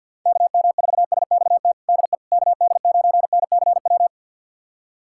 Morse code, heard as a keyed tone, about 51 wpm, 700 Hz. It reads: KM4SXT DE KD9NCW
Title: USA Callsign DE Callsign - 6 Characters - 50wpm